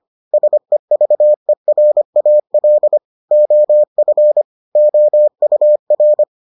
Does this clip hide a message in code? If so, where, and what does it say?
Yes: Morse code from 0.3 to 6.2 s, heard as a keyed tone, about 25 wpm, 600 Hz. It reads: SEVERAL OF OUR